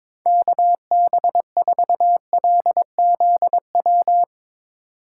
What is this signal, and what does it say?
Morse code, heard as a keyed tone, about 22 wpm, 700 Hz. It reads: KB4LZW
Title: USA Call Signs - 6 Characters - 4x Character Spacing - 22wpm